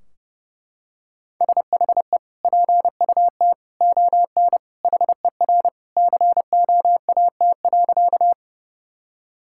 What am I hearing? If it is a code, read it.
Morse code, heard as a keyed tone, about 28 wpm, 700 Hz. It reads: SHE PUT ON HER COAT.
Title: Sentences from Top 1600 Words 30wpm